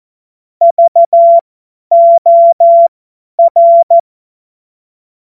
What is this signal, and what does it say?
Morse code, heard as a keyed tone, about 14 wpm, spaced as for 10 wpm, 680 Hz. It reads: VOR
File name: Morsekurs / Witze / Witz 10. August 2021, 50 F70